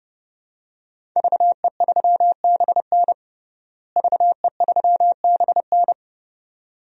Morse code, heard as a keyed tone, about 30 wpm, 700 Hz. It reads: VE3BN VE3BN
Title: DXCC-100 Call Signs - 5 Characters - Repeated 2x - 30wpm